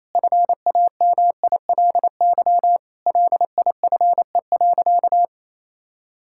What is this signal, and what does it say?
Morse code, heard as a keyed tone, about 28 wpm, 700 Hz. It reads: FAMILY LIFE.
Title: Sentences from Top 1000 Words - Review Part 5 - 28wpm